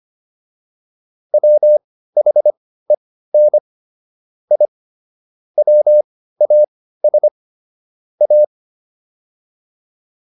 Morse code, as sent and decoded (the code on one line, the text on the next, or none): .-- .... -. / .. / .-- .- ... / .-
WHEN I WAS A